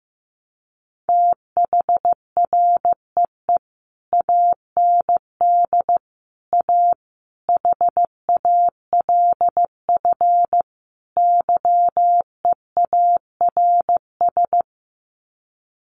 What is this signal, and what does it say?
Morse code, heard as a keyed tone, about 15 wpm, 700 Hz. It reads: THREE AND A HALF YEARS
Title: Sets of 5 Words Encore2 15wpm